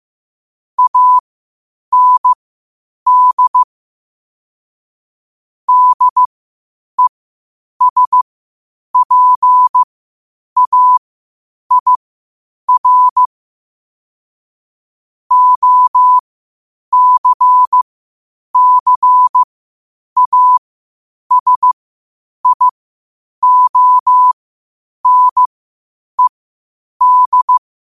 Morse code, heard as a keyed tone, about 15 wpm, spaced as for 8 wpm, 1000 Hz. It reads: AND DESPAIR OCCASIONED
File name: qotdfiles / 8wpm